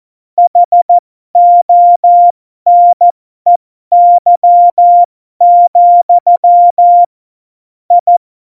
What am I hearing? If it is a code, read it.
Morse code, heard as a keyed tone, about 14 wpm, spaced as for 12 wpm, 700 Hz. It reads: HONEY, I
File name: Morsekurs / Witze